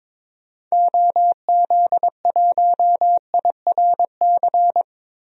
Morse code, heard as a keyed tone, about 21 wpm, 700 Hz. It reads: OZ1IRC
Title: DXCC-100 Call Signs - 6 Characters - 22wpm